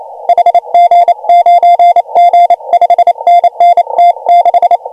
Morse code, heard as a keyed tone, about 30 wpm, 690 Hz. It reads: HG9G 5NN T6
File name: AUDIO2019 / HG1G CW